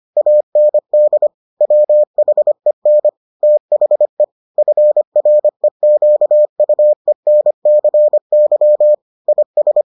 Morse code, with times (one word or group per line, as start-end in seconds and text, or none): 0.2-1.3 AND
1.6-3.1 WHEN
3.4-4.2 THE
4.6-8.9 FREQUENCY
9.3-9.8 IS